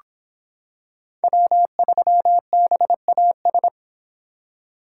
Morse code, heard as a keyed tone, about 25 wpm, 700 Hz. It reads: W3BAS